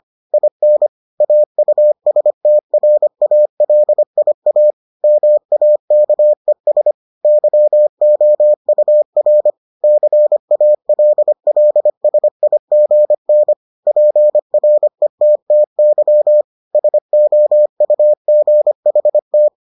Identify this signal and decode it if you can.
Morse code, heard as a keyed tone, about 25 wpm, 600 Hz. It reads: IN AUSTRALIA MAKES YOUR CALLSIGN PRETTY SOUGHT